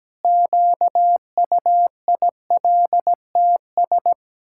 Morse code, heard as a keyed tone, about 17 wpm, 700 Hz. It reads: QUILTS
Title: Pangram Challenge v2 - 17wpm